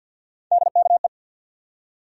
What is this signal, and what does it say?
Morse code, heard as a keyed tone, about 53 wpm, 700 Hz. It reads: DKE
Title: USA Call Sign Postfixes - 3 Letters 50wpm